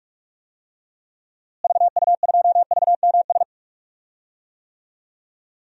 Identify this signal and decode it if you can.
Morse code, heard as a keyed tone, about 46 wpm, 700 Hz. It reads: VU2VMS
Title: DXCC-100 Call Signs - 6 Characters - Repeated 2x - 45wpm